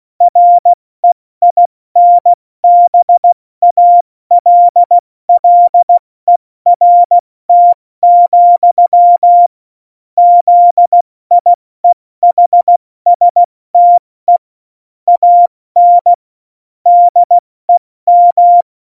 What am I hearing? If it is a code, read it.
Morse code, heard as a keyed tone, about 16 wpm, 700 Hz. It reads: REINBALLERT, ZIEHSTE AN DEM
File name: Morsekurs / Witze / Witz 08. Dezember 2020, 70 F80